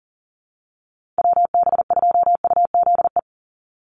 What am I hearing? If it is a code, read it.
Morse code, heard as a keyed tone, about 38 wpm, 700 Hz. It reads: WB2UZE
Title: Call Signs - 40wpm